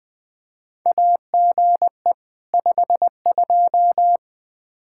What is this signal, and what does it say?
Morse code, heard as a keyed tone, about 21 wpm, 700 Hz. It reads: AGE 52
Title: Practice QSO - Exchange 3 - More Info - 20wpm